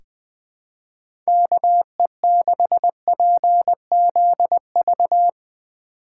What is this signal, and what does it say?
Morse code, heard as a keyed tone, about 21 wpm, 700 Hz. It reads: KE6PZV
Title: USA Call Signs - 6 Characters - 2x Character Spacing - 20wpm